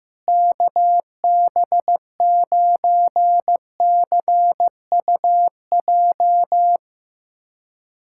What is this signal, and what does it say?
Morse code, heard as a keyed tone, about 15 wpm, 700 Hz. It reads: KB9CUJ
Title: USA Callsign DE Callsign - 6 Characters - 15wpm